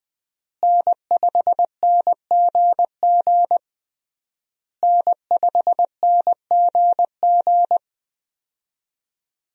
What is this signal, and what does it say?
Morse code, heard as a keyed tone, about 20 wpm, 700 Hz. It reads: N5NGG N5NGG
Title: USA Call Signs - 5 Characters - Repeated 3x - 20wpm